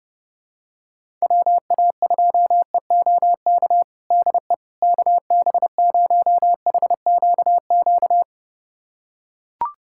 Morse code, heard as a keyed tone, about 32 wpm, 700 Hz. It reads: WA2EOK DE KB0HQQ
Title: USA Callsign DE Callsign - 6 Characters - 30wpm